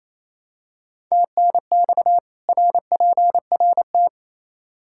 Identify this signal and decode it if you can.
Morse code, heard as a keyed tone, about 28 wpm, 700 Hz. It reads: TNX RPRT
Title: Practice QSO - Exchange 2 - Info - 28wpm